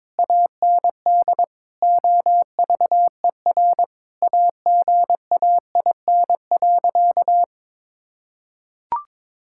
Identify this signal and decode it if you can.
Morse code, heard as a keyed tone, about 22 wpm, 700 Hz. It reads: AND OVER AGAIN.
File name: practice / 22wpm / Sentences with 4-Word Phrases - 22wpm